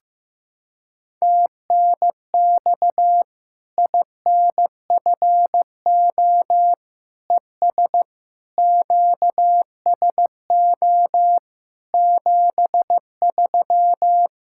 Morse code, heard as a keyed tone, about 15 wpm, 700 Hz. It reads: TNX INFO ES QSO 73
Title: Practice QSO - Exchange 4 - Ending - 15wpm